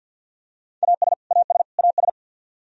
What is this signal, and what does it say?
Morse code, heard as a keyed tone, about 47 wpm, 700 Hz. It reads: AS AS AS
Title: Top 100 Words Repeated 50wpm